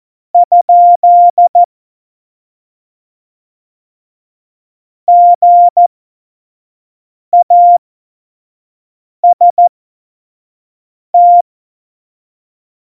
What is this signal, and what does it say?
Morse code, heard as a keyed tone, about 14 wpm, spaced as for 5 wpm, 700 Hz. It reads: ? GAST
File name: Morsekurs / Witze / Witz 29. Oktober 2019, 25 F70